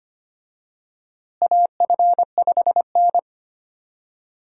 Morse code, heard as a keyed tone, about 25 wpm, 700 Hz. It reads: AF5N
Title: USA Call Signs - 4 Characters - Repeated 2x - 25wpm